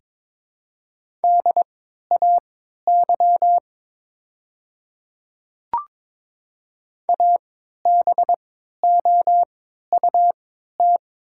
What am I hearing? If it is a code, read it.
Morse code, heard as a keyed tone, about 22 wpm, spaced as for 12 wpm, 700 Hz. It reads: DAY ABOUT
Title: 3x Character Spacing - Top 100 Words - 22wpm